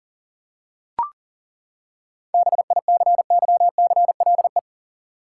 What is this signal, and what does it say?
Morse code, heard as a keyed tone, about 40 wpm, 700 Hz. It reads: BICYCLE
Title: Seven Letter Words - 40wpm